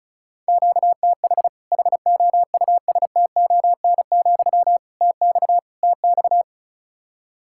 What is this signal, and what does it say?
Morse code, heard as a keyed tone, about 36 wpm, 700 Hz. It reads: QTH HOUSTON, TX TX
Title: Practice QSO - Exchange 1 - Beginning - 35wpm